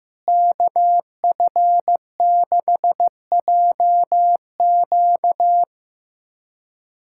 Morse code, heard as a keyed tone, about 15 wpm, 700 Hz. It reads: KF6JQ
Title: USA Callsign DE Callsign - 5 Characters - 15wpm